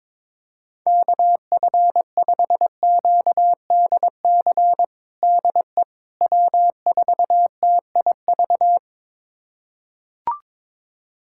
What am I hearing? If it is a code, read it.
Morse code, heard as a keyed tone, about 22 wpm, 700 Hz. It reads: KF5QDC DE W4TIV
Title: USA Callsign DE Callsign 22wpm